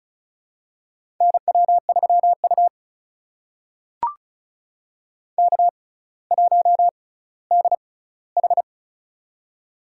Morse code, heard as a keyed tone, about 34 wpm, spaced as for 18 wpm, 700 Hz. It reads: NW3U K1DH
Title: USA Call Signs - 4 Characters - 6x Character Spacing - 35wpm